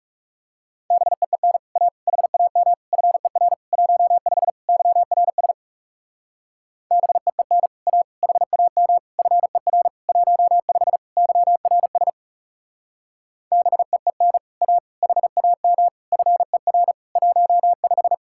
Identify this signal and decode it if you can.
Morse code, heard as a keyed tone, about 35 wpm, 700 Hz. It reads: BEEN A HAM FER 15 YRS BEEN A HAM FER 15 YRS BEEN A HAM FER 15
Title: Speed-Racing Common QSO Phrases - 30wpm